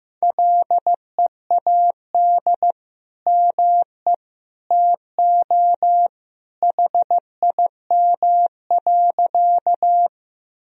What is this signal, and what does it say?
Morse code, heard as a keyed tone, about 15 wpm, 700 Hz. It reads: LEAD ME TO HIM.
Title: Sentences from Top 1000 Words - Review Part 2 - 15wpm